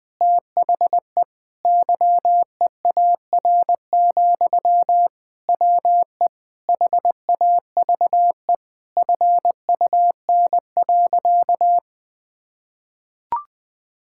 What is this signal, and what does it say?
Morse code, heard as a keyed tone, about 20 wpm, 700 Hz. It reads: THE YEAR, WE HAVE FUN.
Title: Sentences from Top 1000 Words - Review Part 5 - 20wpm